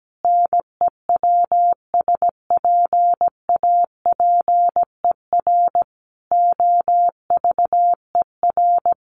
Morse code, heard as a keyed tone, about 17 wpm, 700 Hz. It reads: NEWSPAPER OVER
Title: Sentences from Top 1100 Words 17wpm